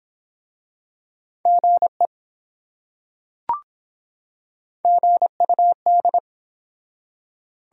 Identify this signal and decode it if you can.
Morse code, heard as a keyed tone, about 24 wpm, 700 Hz. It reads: GE GUD